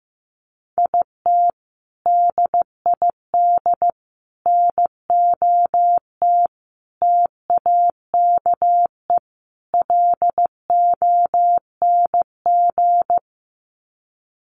Morse code, heard as a keyed tone, about 15 wpm, 700 Hz. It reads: IT DID NOT TAKE LONG
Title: Sets of 5 Words Encore2 15wpm